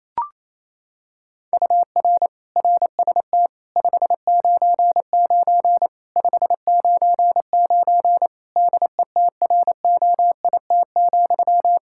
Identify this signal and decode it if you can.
Morse code, heard as a keyed tone, about 28 wpm, 700 Hz. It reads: UR RST 599 599 DETROIT,
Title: Practice QSO - Exchange 1 - Beginning - 28wpm